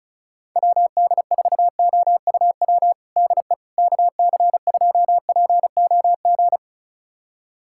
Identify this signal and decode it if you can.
Morse code, heard as a keyed tone, about 36 wpm, 700 Hz. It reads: WD4OUW DE KC2POG
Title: USA Callsign DE Callsign 35wpm